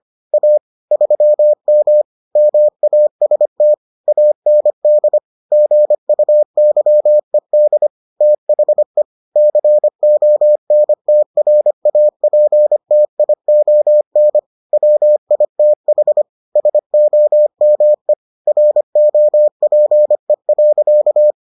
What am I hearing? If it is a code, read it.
Morse code, heard as a keyed tone, about 25 wpm, 600 Hz. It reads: A 3M MAST AND GUYED THE CONTRAPTION WITH SOME ROPE.